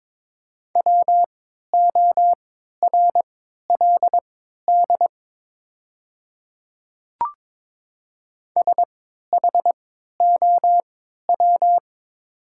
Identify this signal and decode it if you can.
Morse code, heard as a keyed tone, about 21 wpm, spaced as for 12 wpm, 700 Hz. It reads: WORLD SHOW
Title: 3x Character Spacing - Top 200 Words - 22wpm